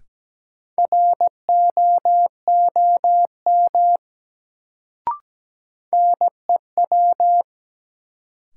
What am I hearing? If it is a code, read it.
Morse code, heard as a keyed tone, about 17 wpm, 700 Hz. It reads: ROOM NEW
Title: Sentences from Top 300 Words - New Word Drill - 17wpm